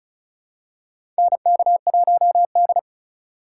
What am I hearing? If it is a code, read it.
Morse code, heard as a keyed tone, about 36 wpm, 700 Hz. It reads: NK1D
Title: USA Call Signs - 4 Characters - 3x Character Spacing - 35wpm